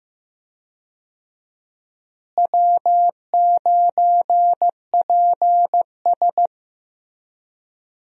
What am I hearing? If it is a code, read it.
Morse code, heard as a keyed tone, about 15 wpm, 700 Hz. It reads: W9PS
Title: USA Call Signs - 4 Characters - Repeated 3x - 15wpm